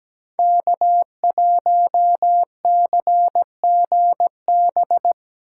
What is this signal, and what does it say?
Morse code, heard as a keyed tone, about 17 wpm, 700 Hz. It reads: K1CGB
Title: Code Groups - 5 Letter:Numbers - 17wpm